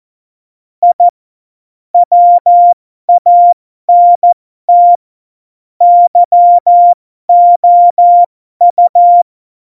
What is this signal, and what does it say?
Morse code, heard as a keyed tone, about 14 wpm, spaced as for 12 wpm, 700 Hz. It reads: I WANT YOU
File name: Morsekurs / Witze